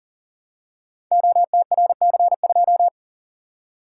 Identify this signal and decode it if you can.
Morse code, heard as a keyed tone, about 42 wpm, 700 Hz. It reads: OTRC2
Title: Code Groups - 5 Letter:Numbers - 40wpm